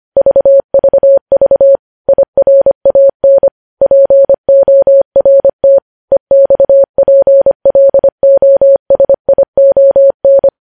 Morse code, heard as a keyed tone, about 25 wpm, 550 Hz. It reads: VVV IRAN PORT EXPLOSION
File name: podcast / News headlines-1x-25x25